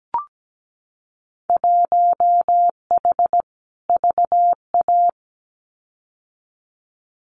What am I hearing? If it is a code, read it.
Morse code, heard as a keyed tone, about 17 wpm, 700 Hz. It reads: 1H VA
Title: Winter Field Day Exchanges 17wpm